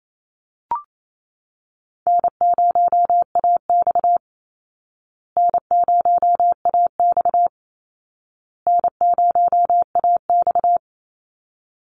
Morse code, heard as a keyed tone, about 28 wpm, 700 Hz. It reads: N0AX N0AX N0AX
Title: Call Signs Repeated 2x - 28wpm